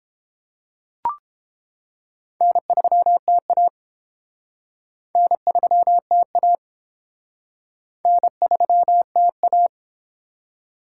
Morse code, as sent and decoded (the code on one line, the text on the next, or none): -. ...-- - .- / -. ...-- - .- / -. ...-- - .-
N3TA N3TA N3TA